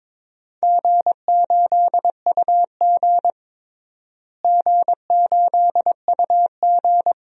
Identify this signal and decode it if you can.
Morse code, heard as a keyed tone, about 22 wpm, 700 Hz. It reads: G8UG G8UG
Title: DXCC-100 Call Signs - 4 Characters - Repeated 2x - 22wpm